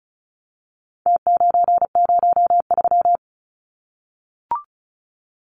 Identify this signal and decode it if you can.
Morse code, heard as a keyed tone, about 34 wpm, 700 Hz. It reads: T903
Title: Numbers - Contest Style - 4-Digit 35wpm